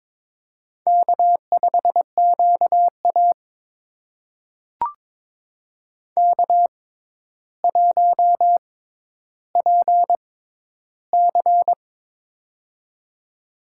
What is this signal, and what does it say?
Morse code, heard as a keyed tone, about 22 wpm, spaced as for 11 wpm, 700 Hz. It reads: K5QA K1PC
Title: USA Call Signs - 4 Characters - 6x Character Spacing - 22wpm